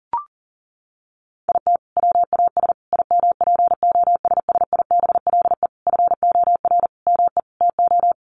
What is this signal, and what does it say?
Morse code, heard as a keyed tone, about 40 wpm, 700 Hz. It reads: IT WAS IMPOSSIBLE FOR ME TO
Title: Sentences from Top 1100 Words 40wpm